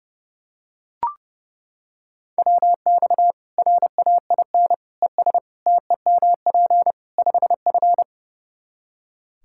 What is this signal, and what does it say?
Morse code, heard as a keyed tone, about 30 wpm, 700 Hz. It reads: WX RAIN ES TEMP 5F
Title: K1USN SST QSO Practice 30wpm